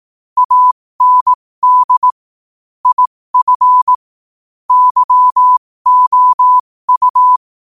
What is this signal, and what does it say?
Morse code, heard as a keyed tone, about 18 wpm, spaced as for 15 wpm, 1000 Hz. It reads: AND IF YOU
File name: qotdfiles / 15wpm